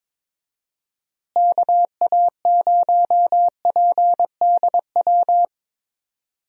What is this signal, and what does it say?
Morse code, heard as a keyed tone, about 22 wpm, 700 Hz. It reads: KA0PDW
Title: USA Call Signs - 6 Characters - Repeated 3x - 22wpm